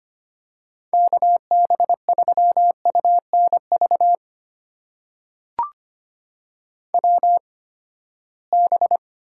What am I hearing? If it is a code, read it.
Morse code, heard as a keyed tone, about 25 wpm, spaced as for 17 wpm, 700 Hz. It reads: KB3UNV WB
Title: USA Call Signs - 6 Characters - 8x Character Spacing - 25wpm